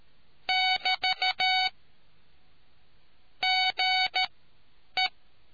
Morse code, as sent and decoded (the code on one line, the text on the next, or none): -...- / --. .
<BT> GE